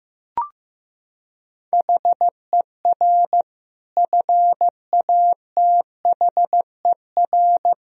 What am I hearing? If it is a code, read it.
Morse code, heard as a keyed tone, about 15 wpm, 700 Hz. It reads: HER FATHER
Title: Sentences from Top 1000 Words - Review Part 3 - 15wpm